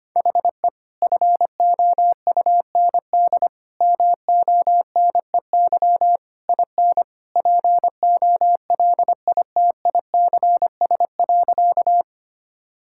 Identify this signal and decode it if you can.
Morse code, heard as a keyed tone, about 25 wpm, 700 Hz. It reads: HE FOUND MONEY IN POLITICS.
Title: Sentences from Top 1000 Words - Review Part 6 - 25wpm